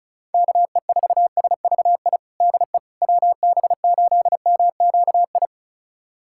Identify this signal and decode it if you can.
Morse code, heard as a keyed tone, about 34 wpm, 700 Hz. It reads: KE4SVI DE WB8MQI
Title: USA Callsign DE Callsign - 6 Characters - 35wpm